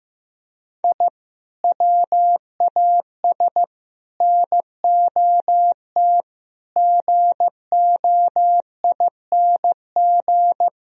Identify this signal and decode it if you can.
Morse code, heard as a keyed tone, about 15 wpm, 700 Hz. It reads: I WAS NOT GOING
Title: Speed-Racing - Sets of 5 Words 15wpm